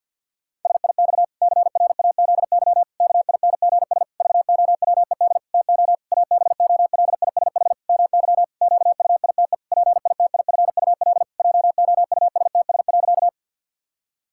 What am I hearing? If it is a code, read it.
Morse code, heard as a keyed tone, about 49 wpm, 700 Hz. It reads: SIX CRAZY KINGS VOWED TO ABOLISH MY QUITE PITIFUL JOUSTS.
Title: Pangram Challenge v2 - 50wpm